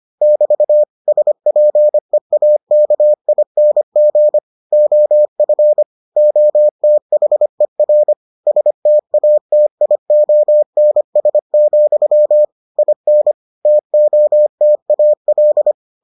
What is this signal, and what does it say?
Morse code, heard as a keyed tone, about 25 wpm, 600 Hz. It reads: <BT> SPEAKING OF OTHER STATIONS, IN TOTAL